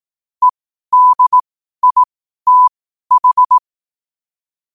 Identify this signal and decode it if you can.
Morse code, heard as a keyed tone, about 18 wpm, spaced as for 12 wpm, 1000 Hz. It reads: EDITH
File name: qotdfiles / 12wpm